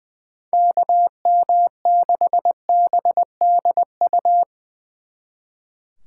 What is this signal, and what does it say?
Morse code, heard as a keyed tone, about 23 wpm, 700 Hz. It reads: KM6BDU
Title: Speed Racing - USA 2x3 Call Signs 20wpm